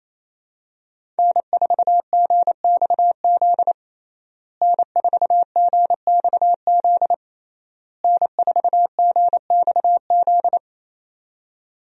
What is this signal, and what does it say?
Morse code, heard as a keyed tone, about 27 wpm, 700 Hz. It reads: N4GXZ N4GXZ N4GXZ
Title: USA Call Signs - 5 Characters - Repeated 3x - 28wpm